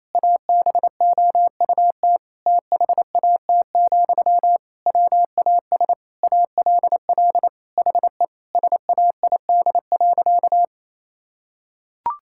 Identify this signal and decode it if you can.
Morse code, heard as a keyed tone, about 28 wpm, 700 Hz. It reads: ABOUT THAT, WAS ALL HE SAID.
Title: Sentences from Top 1600 Words 28wpm